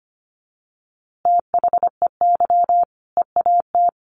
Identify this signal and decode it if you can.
Morse code, heard as a keyed tone, about 25 wpm, 700 Hz. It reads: THEY EAT
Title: Easy Phrases 25wpm